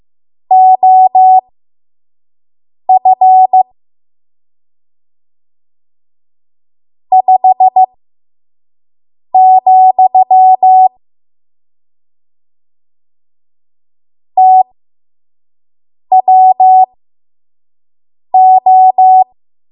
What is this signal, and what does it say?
Morse code, heard as a keyed tone, about 15 wpm, spaced as for 5 wpm, 750 Hz. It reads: OF 5, TWO
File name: Morse / 05WPM